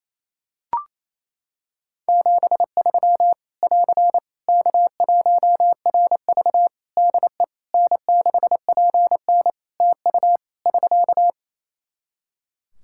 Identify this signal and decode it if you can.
Morse code, heard as a keyed tone, about 28 wpm, 700 Hz. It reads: 73 <AR> K1RV DE N6PN TU <SK>
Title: Practice QSO - Exchange 4 - Ending - 28wpm